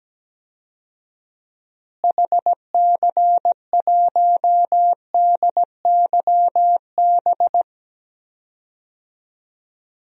Morse code, heard as a keyed tone, about 17 wpm, 700 Hz. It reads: HC1DYB
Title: DXCC-100 Call Signs - 6 Characters - Repeated 2x - 17wpm